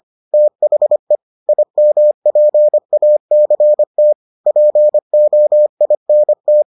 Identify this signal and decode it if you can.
Morse code, heard as a keyed tone, about 25 wpm, 600 Hz. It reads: THE IMPACT POINT